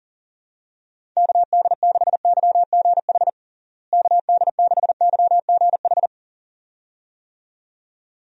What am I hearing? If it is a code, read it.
Morse code, heard as a keyed tone, about 40 wpm, 700 Hz. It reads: KD6YGH KD6YGH
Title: USA Call Signs - 6 Characters - Repeated 2x - 40wpm